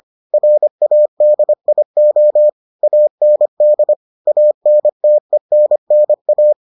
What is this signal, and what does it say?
Morse code, heard as a keyed tone, about 25 wpm, 600 Hz. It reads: RADIO AND ANTENNA